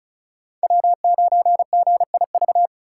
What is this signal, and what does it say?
Morse code, heard as a keyed tone, about 47 wpm, 700 Hz. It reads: W9GIV